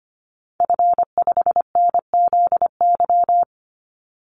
Morse code, heard as a keyed tone, about 25 wpm, 700 Hz. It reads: F5NZY
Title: Call Signs Repeated 2x - 25wpm